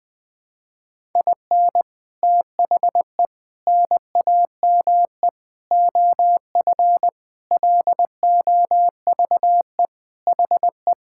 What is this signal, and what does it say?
Morse code, heard as a keyed tone, about 20 wpm, 700 Hz. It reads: IN THE NAME OF LOVE HE
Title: Sentences with 4-Word Phrases - 20wpm